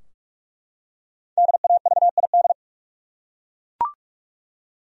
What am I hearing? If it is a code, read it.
Morse code, heard as a keyed tone, about 43 wpm, 700 Hz. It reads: DAVID
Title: USA Common Names - Max 5 Letters - 45wpm